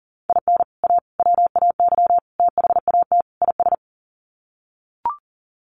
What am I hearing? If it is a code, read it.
Morse code, heard as a keyed tone, about 40 wpm, 700 Hz. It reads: IN A WAY THAT IS
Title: Sets of 5 Words Encore 40wpm